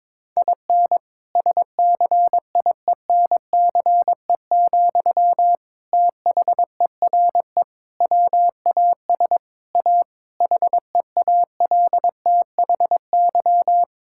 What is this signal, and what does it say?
Morse code, heard as a keyed tone, about 22 wpm, 700 Hz. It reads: IN SCIENCE, THERE WAS A HEALTHY
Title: Sentences from Top 1000 Words - Review Part 5 - 22wpm